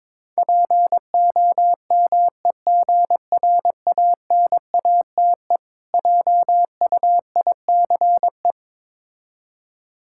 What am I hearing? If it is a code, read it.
Morse code, heard as a keyed tone, about 23 wpm, 690 Hz. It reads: POMEGRANATE JUICE
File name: Difficult Words with Hints - 22wpm